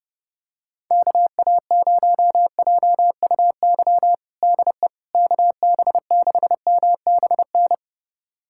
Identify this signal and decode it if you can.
Morse code, heard as a keyed tone, about 32 wpm, 700 Hz. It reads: KA0JUY DE KB6MBN
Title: USA Callsign DE Callsign 30wpm